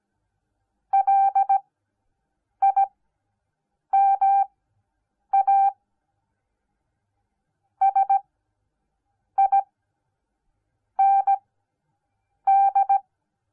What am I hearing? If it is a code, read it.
Morse code, heard as a keyed tone, about 17 wpm, spaced as for 6 wpm, 790 Hz. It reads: LIMA SIND